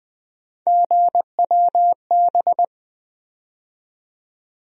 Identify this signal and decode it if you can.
Morse code, heard as a keyed tone, about 20 wpm, 700 Hz. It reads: GWB
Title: USA Call Sign Postfixes - 3 Letters 20wpm